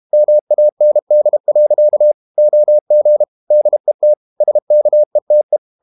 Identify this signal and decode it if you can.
Morse code, heard as a keyed tone, about 32 wpm, 600 Hz. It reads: MAND. OG DET SKETE
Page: Ferie 32 wpm